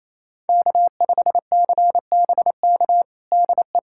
Morse code, heard as a keyed tone, about 28 wpm, 700 Hz. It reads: K5CBK DE
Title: USA Callsign DE Callsign 28wpm